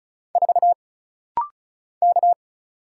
Morse code, heard as a keyed tone, about 36 wpm, 700 Hz. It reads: 4 K
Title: Single Letter-Number Alt - Rapid-Fire 35wpm